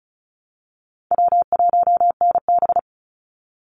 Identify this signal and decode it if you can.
Morse code, heard as a keyed tone, about 34 wpm, 700 Hz. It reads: W1NB
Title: USA Call Signs 35wpm